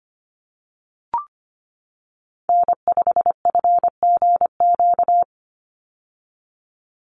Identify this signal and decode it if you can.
Morse code, heard as a keyed tone, about 25 wpm, 700 Hz. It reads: N5FGQ
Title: USA Call Signs 25wpm